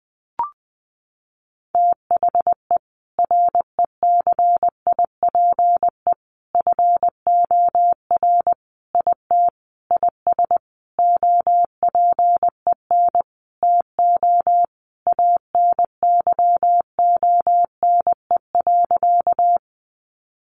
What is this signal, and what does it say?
Morse code, heard as a keyed tone, about 20 wpm, 700 Hz. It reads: THE RECIPE FOR IT IS OPEN TO ANYONE.
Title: Sentences from Top 1200 Words 20wpm